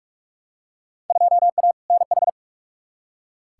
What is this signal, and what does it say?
Morse code, heard as a keyed tone, about 46 wpm, 700 Hz. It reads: 2A NH
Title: Field Day Exchanges Encore - 45wpm